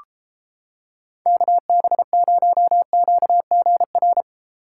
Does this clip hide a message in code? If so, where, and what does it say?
Yes: Morse code from 1.3 to 4.2 s, heard as a keyed tone, about 31 wpm, 700 Hz. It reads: KB0QGR